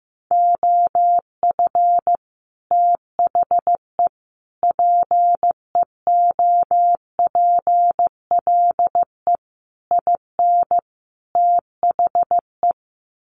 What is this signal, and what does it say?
Morse code, heard as a keyed tone, about 15 wpm, 700 Hz. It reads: OF THE PEOPLE IN THE
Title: Sets of 5 Words Encore 15wpm